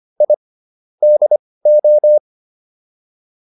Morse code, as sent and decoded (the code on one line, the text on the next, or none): .. / -.. ---
I DO